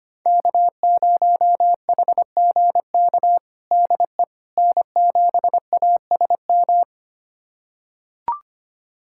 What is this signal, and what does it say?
Morse code, heard as a keyed tone, about 25 wpm, 700 Hz. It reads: K0HGK DE N7ASM
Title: USA Callsign DE Callsign - 5 Characters - 25wpm